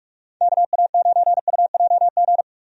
Morse code, heard as a keyed tone, about 42 wpm, 700 Hz. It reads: KA9UJG
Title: USA Call Signs - 6 Characters - 45wpm